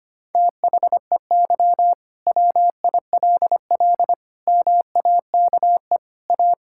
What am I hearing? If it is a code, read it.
Morse code, heard as a keyed tone, about 25 wpm, 700 Hz. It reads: THEY WILL MAKE A
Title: Sentences from Top 1000 Words - Review Part 5 - 25wpm